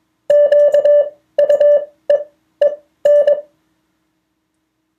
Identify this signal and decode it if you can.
Morse code, heard as a keyed tone, about 21 wpm, spaced as for 14 wpm, 580 Hz. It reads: QUEEN
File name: 2of4-words-1